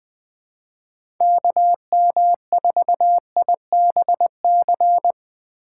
Morse code, heard as a keyed tone, about 21 wpm, 700 Hz. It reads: KM4IBC
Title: USA Call Signs - 2x Character Spacing - 20wpm